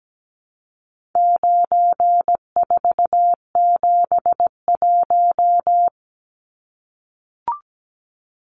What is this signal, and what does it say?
Morse code, heard as a keyed tone, about 17 wpm, 700 Hz. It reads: 9471
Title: Numbers - 4-Digit 17wpm